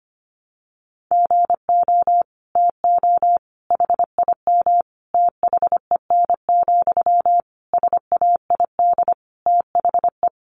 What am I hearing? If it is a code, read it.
Morse code, heard as a keyed tone, about 25 wpm, 700 Hz. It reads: GO TO HIM THEN, SAID THE
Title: Sentences from Top 1100 Words 25wpm